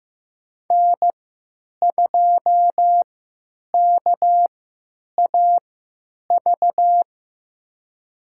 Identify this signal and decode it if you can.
Morse code, heard as a keyed tone, about 15 wpm, spaced as for 9 wpm, 700 Hz. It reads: N2KAV
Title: USA Call Signs - 3x Character Spacing - 15wpm